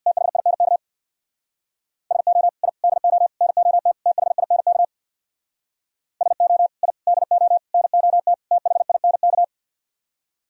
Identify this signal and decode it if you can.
Morse code, heard as a keyed tone, about 59 wpm, 690 Hz. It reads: THINK SO I DO NOT THINK SO I DO NOT THINK
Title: Speed-Racing - Sets of 5 Words 50wpm